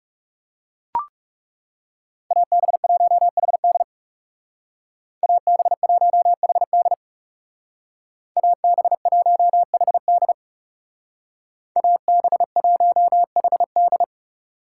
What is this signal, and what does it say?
Morse code, heard as a keyed tone, about 39 wpm, 700 Hz. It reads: AB1HD AB1HD AB1HD AB1HD